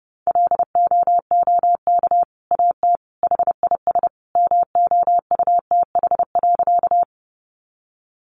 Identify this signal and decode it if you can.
Morse code, heard as a keyed tone, about 30 wpm, 700 Hz. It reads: LOOK AT HIS MOUTH.
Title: Sentences from Top 1500 Words 30wpm